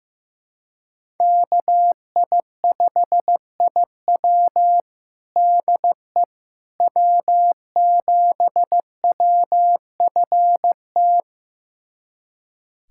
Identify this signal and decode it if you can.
Morse code, heard as a keyed tone, about 16 wpm, 700 Hz. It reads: KI5IW DE W7WFT
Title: USA Callsign DE Callsign 15wpm